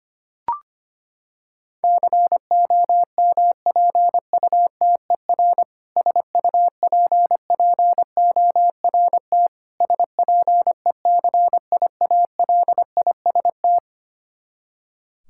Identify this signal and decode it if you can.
Morse code, heard as a keyed tone, about 25 wpm, 700 Hz. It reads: COMPUTER SUPPORT SPECIALIST
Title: Practice QSO - Exchange 3 - More Info - 25wpm